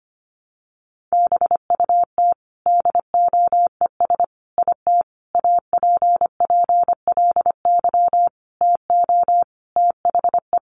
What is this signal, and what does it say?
Morse code, heard as a keyed tone, about 25 wpm, 700 Hz. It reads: BUT DOES IT APPLY TO THE
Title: Sentences from Top 1200 Words 25wpm